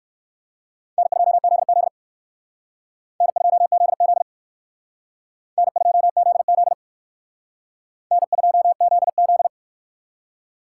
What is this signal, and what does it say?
Morse code, heard as a keyed tone, about 56 wpm, 700 Hz. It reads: N2ZZ N2ZZ N2ZZ N2ZZ